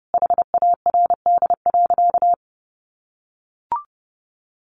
Morse code, heard as a keyed tone, about 30 wpm, 700 Hz. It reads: HARD.
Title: Additional Easy Phrases 30wpm